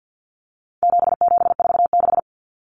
Morse code, heard as a keyed tone, about 50 wpm, 700 Hz. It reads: 7746
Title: Numbers - 4-Digit 50wpm